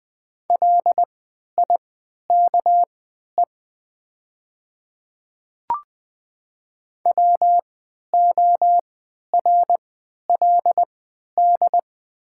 Morse code, heard as a keyed tone, about 20 wpm, spaced as for 11 wpm, 700 Hz. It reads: LIKE WORLD
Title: 3x Character Spacing - Top 300 Words - 20wpm